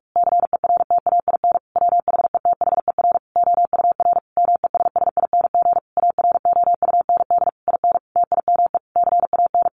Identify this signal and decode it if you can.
Morse code, heard as a keyed tone, about 44 wpm, 700 Hz. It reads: CERTAIN WHETHER OUR MESSING AROUND IN TIME CAN
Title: A Sound Of Thunder 45wpm